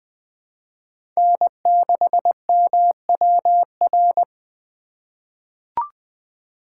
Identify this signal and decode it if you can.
Morse code, heard as a keyed tone, about 20 wpm, 700 Hz. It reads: N6MWR
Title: USA Call Signs - 5 Characters - 3x Character Spacing - 20wpm